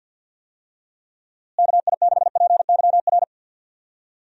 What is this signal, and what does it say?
Morse code, heard as a keyed tone, about 48 wpm, 700 Hz. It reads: KI6PYR
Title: USA Call Signs - 6 Characters - Repeated 2x - 50wpm